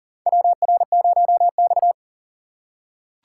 Morse code, heard as a keyed tone, about 40 wpm, 700 Hz. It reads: WR0X
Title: USA Call Signs - 4 Characters - 2x Character Spacing - 40wpm